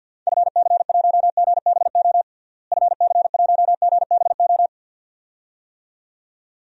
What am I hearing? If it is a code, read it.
Morse code, heard as a keyed tone, about 49 wpm, 700 Hz. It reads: FC1GBO FC1GBO
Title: DXCC-100 Call Signs - 6 Characters - Repeated 2x - 50wpm